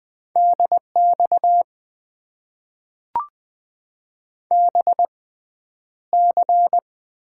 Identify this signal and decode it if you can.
Morse code, heard as a keyed tone, about 20 wpm, spaced as for 10 wpm, 700 Hz. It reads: DX BC
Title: Top 90 QSO Elements - 6x Character Spacing - 20wpm